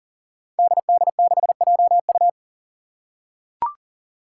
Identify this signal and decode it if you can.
Morse code, heard as a keyed tone, about 40 wpm, 700 Hz. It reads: DD6JU
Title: Five Challenging Letter-Number Combinations 40wpm